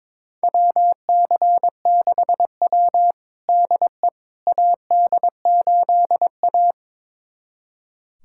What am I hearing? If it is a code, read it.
Morse code, heard as a keyed tone, about 22 wpm, 700 Hz. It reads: WC6W DE AD8A
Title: USA Callsign DE Callsign - 4 Characters - 22wpm